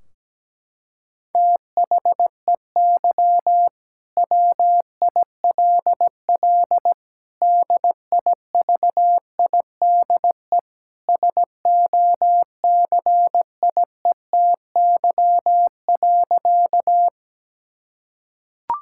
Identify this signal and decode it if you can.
Morse code, heard as a keyed tone, about 17 wpm, 700 Hz. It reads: THEY WILL DIVIDE SOCIETY.
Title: Sentences from Top 1600 Words 17wpm